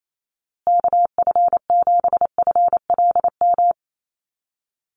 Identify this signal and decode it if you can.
Morse code, heard as a keyed tone, about 28 wpm, 700 Hz. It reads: KF7FLM
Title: USA Call Signs Repeated 3x 28wpm